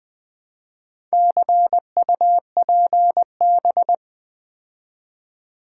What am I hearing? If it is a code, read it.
Morse code, heard as a keyed tone, about 20 wpm, 700 Hz. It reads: CUPB
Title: Build A Word - 2 Letters At A Time - 8-Letter Words - 20wpm